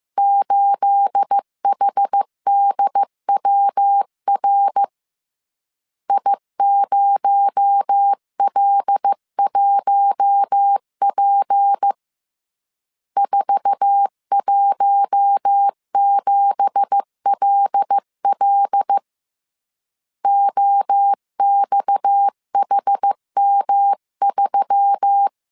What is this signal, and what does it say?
Morse code, heard as a keyed tone, about 15 wpm, 800 Hz. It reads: 8HDWR I0L1P 417LL OXHM3
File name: CW / DET / 1 9 1500C S A 14wpm 3222